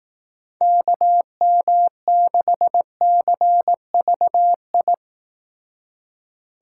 Morse code, heard as a keyed tone, about 19 wpm, 700 Hz. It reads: KM6CVI
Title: Speed Racing - USA 2x3 Call Signs 15wpm